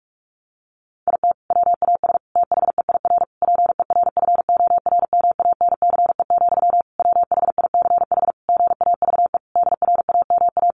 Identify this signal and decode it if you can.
Morse code, heard as a keyed tone, about 44 wpm, 700 Hz. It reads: IT WAS THEIR PERFORMANCE, WHICH GAVE DRAMA
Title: Sentences from Top 1500 Words 45wpm